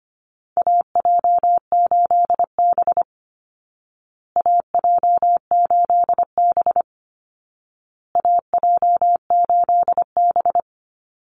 Morse code, heard as a keyed tone, about 25 wpm, 700 Hz. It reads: AJ8B AJ8B AJ8B
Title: Call Signs Repeated 2x - 25wpm